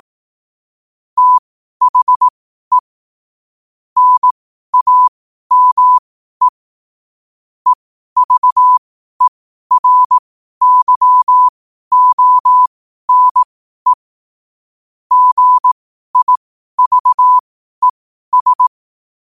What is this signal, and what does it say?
Morse code, heard as a keyed tone, about 18 wpm, spaced as for 12 wpm, 1000 Hz. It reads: THE NAME EVERYONE GIVES
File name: qotdfiles / 12wpm